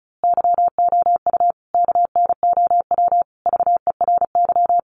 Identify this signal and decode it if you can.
Morse code, heard as a keyed tone, about 34 wpm, 700 Hz. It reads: YOU KNOW VERY
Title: A Sound Of Thunder 35wpm